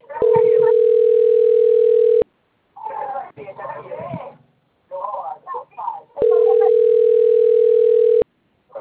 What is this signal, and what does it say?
Ringback tone (North America), 440 + 480 Hz, with speech.
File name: out- - - - - .60808